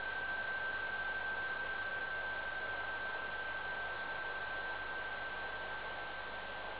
PSK31, about 31.25 baud.